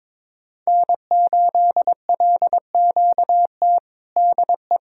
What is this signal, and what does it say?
Morse code, heard as a keyed tone, about 22 wpm, 700 Hz. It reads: N8LQT DE
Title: Practice QSO - Exchange 4 - Ending - 22wpm